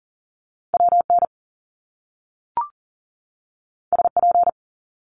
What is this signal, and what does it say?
Morse code, heard as a keyed tone, about 40 wpm, 700 Hz. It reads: WN SP
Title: Top Two Letter Combinations in Words 40wpm